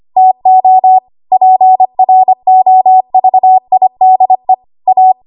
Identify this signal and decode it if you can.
Morse code, heard as a keyed tone, about 25 wpm, 750 Hz. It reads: TO PROVIDE A
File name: Archive / 25 WPM / 25WPM